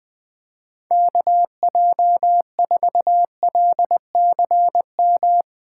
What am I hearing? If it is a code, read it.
Morse code, heard as a keyed tone, about 20 wpm, 700 Hz. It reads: KJ4LCM
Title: USA Call Signs - 6 Characters - 2x Character Spacing - 20wpm